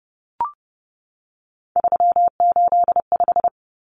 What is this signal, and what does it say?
Morse code, heard as a keyed tone, about 30 wpm, 700 Hz. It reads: 385
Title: Common Cut Numbers - 3-Digit 30wpm